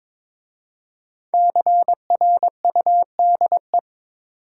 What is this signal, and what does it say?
Morse code, heard as a keyed tone, about 22 wpm, 690 Hz. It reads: CRUDE
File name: 5 Letter Words - First 3 Letters Given - 22wpm